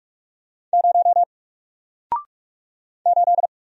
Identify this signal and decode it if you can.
Morse code, heard as a keyed tone, about 44 wpm, 700 Hz. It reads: 0 8
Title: Single Letter-Number Alt 45wpm